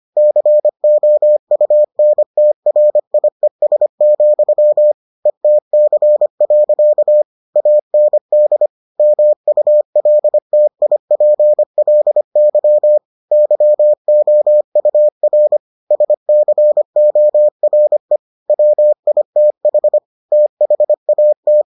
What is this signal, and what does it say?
Morse code, heard as a keyed tone, about 25 wpm, 600 Hz. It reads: COUNTRIES, ETC. AND MULTIPLY YOUR SCORE WITH THAT